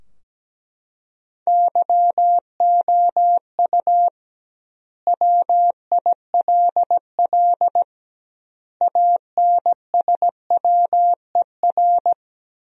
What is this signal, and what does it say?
Morse code, heard as a keyed tone, about 17 wpm, 700 Hz. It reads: YOU WILL ANSWER
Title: 2x Word-Spacing - Sentences from Top 300 Words - 17wpm